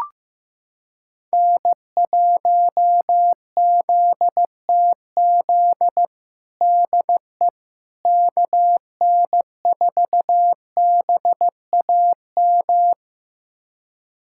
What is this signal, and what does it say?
Morse code, heard as a keyed tone, about 15 wpm, 700 Hz. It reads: N1ZTZ DE KN4BAM
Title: USA Callsign DE Callsign 15wpm